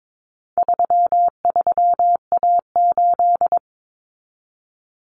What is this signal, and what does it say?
Morse code, heard as a keyed tone, about 22 wpm, 700 Hz. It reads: 33A8
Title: Common Cut Numbers - 4-Digit 22wpm